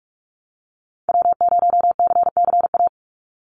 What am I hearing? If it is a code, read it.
Morse code, heard as a keyed tone, about 44 wpm, 700 Hz. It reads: W0CCA
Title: NA SOTA Activator Callsigns 45wpm